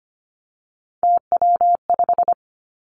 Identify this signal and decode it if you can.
Morse code, heard as a keyed tone, about 25 wpm, 700 Hz. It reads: TW5
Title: Three Challenging Letter-Number Combinations 25wpm